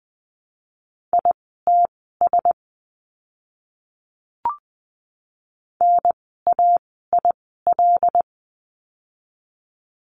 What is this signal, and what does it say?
Morse code, heard as a keyed tone, about 20 wpm, spaced as for 14 wpm, 700 Hz. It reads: ITS NAIL
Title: 075 Lesson 13f - Letter L - Words 20wpm